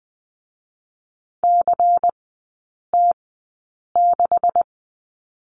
Morse code, heard as a keyed tone, about 20 wpm, spaced as for 8 wpm, 690 Hz. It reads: CT6
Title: 145 Lesson 24e - Letter y - Three Characters 20wpm